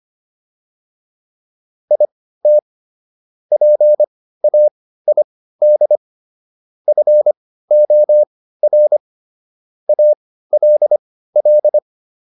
Morse code, heard as a keyed tone, about 25 wpm, spaced as for 15 wpm, 600 Hz. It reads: IT PAID FOR ALL